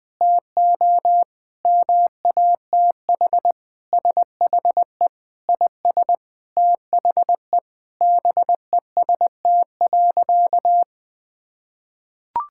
Morse code, heard as a keyed tone, about 21 wpm, 700 Hz. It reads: TO MATH SHE IS THE BEST.
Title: Sentences with 3-Word Phrases - 20wpm